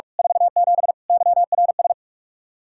Morse code, heard as a keyed tone, about 43 wpm, 700 Hz. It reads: 47 YRS
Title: Practice QSO - Exchange 3 - More Info - 45wpm